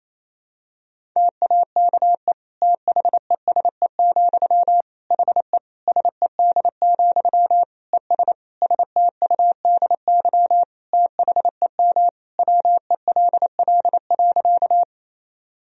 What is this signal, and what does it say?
Morse code, heard as a keyed tone, about 27 wpm, 700 Hz. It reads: TAKE THESE, HE SED, ES STUDY THEM WELL.
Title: Sentences from Top 300 Words and Abbreviations 28wpm